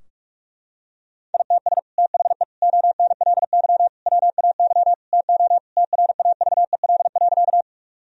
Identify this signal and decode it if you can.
Morse code, heard as a keyed tone, about 46 wpm, 700 Hz. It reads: ITS THE ONLY WAY TO TRAVEL.
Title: Sentences from Top 1000 Words - Review Part 2 - 45wpm